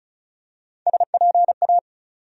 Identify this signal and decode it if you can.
Morse code, heard as a keyed tone, about 34 wpm, 700 Hz. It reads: SPA
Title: USA Call Sign Postfixes - 3 Letters 35wpm